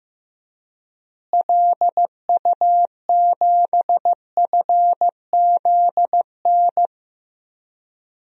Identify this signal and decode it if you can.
Morse code, heard as a keyed tone, about 16 wpm, 700 Hz. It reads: LU7FZN
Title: DXCC-100 Call Signs - 6 Characters - Repeated 2x - 15wpm